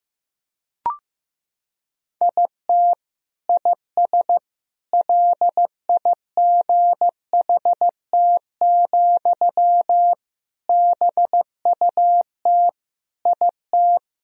Morse code, heard as a keyed tone, about 16 wpm, 700 Hz. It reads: IT IS LIGHT, BUT IT
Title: Sentences from Top 1000 Words - Review Part 1 - 15wpm